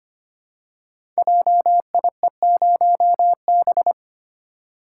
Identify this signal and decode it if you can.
Morse code, heard as a keyed tone, about 25 wpm, 700 Hz. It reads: JIE0B
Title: Code Groups - 5 Letter:Numbers - 25wpm